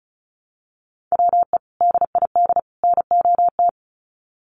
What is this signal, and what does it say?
Morse code, heard as a keyed tone, about 34 wpm, 700 Hz. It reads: WE DID NOT
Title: Sets of 3 Words 35wpm